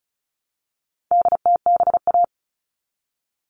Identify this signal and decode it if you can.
Morse code, heard as a keyed tone, about 34 wpm, 700 Hz. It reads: DTBA
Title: All Cut Numbers - 4-Digit 35wpm